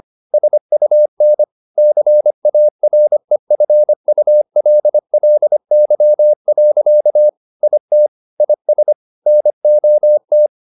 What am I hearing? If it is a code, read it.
Morse code, heard as a keyed tone, about 25 wpm, 600 Hz. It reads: SUN CAREFULLY. IT IS NOT